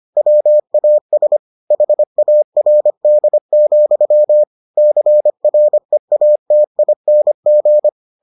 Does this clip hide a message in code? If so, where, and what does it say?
Yes: Morse code from 0.2 to 7.9 s, heard as a keyed tone, about 25 wpm, 600 Hz. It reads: WAS HARD, CREATING